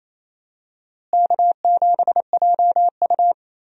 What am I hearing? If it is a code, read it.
Morse code, heard as a keyed tone, about 28 wpm, 700 Hz. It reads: K7JU
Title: USA Call Signs - 4 Characters - Repeated 2x - 28wpm